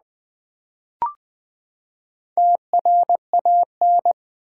Morse code, heard as a keyed tone, about 20 wpm, 700 Hz. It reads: TRAN
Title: Build A Word - 4 Letters At A Time - 11-Letter Words - 20wpm